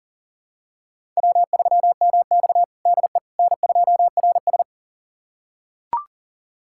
Morse code, heard as a keyed tone, about 40 wpm, 700 Hz. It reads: W3MX DE N2RS
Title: USA Callsign DE Callsign - 4 Characters - 40wpm